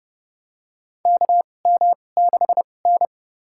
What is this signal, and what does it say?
Morse code, heard as a keyed tone, about 29 wpm, spaced as for 21 wpm, 700 Hz. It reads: KM6N
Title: USA Call Signs - 4 Characters - 2x Character Spacing - 30wpm